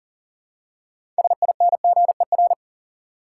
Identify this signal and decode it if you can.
Morse code, heard as a keyed tone, about 40 wpm, 700 Hz. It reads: SINGER
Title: Six Letter Words - 40wpm